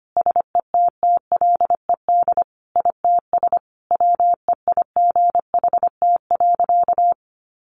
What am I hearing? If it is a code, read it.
Morse code, heard as a keyed tone, about 25 wpm, 700 Hz. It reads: SETTLED ITS WEIGHT.
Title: A Sound Of Thunder 25wpm